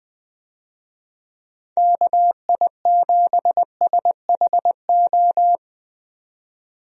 Morse code, heard as a keyed tone, about 20 wpm, 700 Hz. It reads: KI7SHO
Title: USA Call Signs - 6 Characters - 20wpm